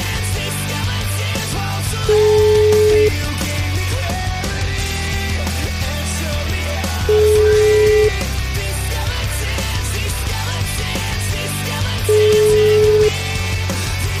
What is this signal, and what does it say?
Ringback tone (Europe, 425 Hz), with noise.